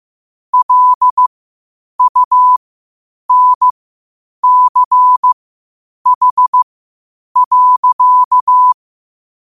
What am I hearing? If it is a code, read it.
Morse code, heard as a keyed tone, about 15 wpm, spaced as for 8 wpm, 1000 Hz. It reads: LUNCH.
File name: qotdfiles / 8wpm